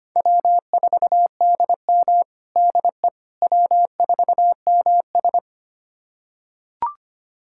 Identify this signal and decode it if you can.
Morse code, heard as a keyed tone, about 25 wpm, 700 Hz. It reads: W4DM DE W4MS
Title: USA Callsign DE Callsign - 4 Characters - 25wpm